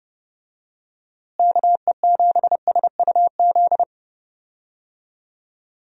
Morse code, heard as a keyed tone, about 29 wpm, 700 Hz. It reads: KE7SUZ
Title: USA Call Signs - 6 Characters - Repeated 2x - 30wpm